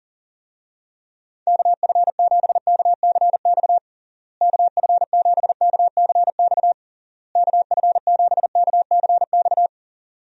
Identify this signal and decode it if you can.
Morse code, heard as a keyed tone, about 37 wpm, 700 Hz. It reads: KF7KCX KF7KCX KF7KCX
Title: USA Call Signs - 6 Characters - Repeated 3x - 40wpm